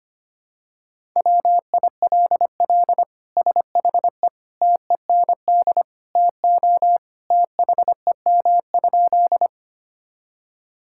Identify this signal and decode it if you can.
Morse code, heard as a keyed tone, about 25 wpm, 700 Hz. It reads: WILL SHE TEND TO THEM?
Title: Sentences from Top 1000 Words - Review Part 5 - 25wpm